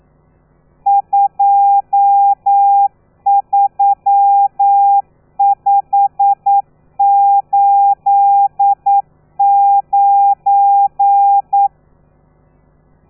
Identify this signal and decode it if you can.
Morse code, heard as a keyed tone, about 9 wpm, 800 Hz. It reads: 23589
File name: telegrafia / i2tfj / Lez12 40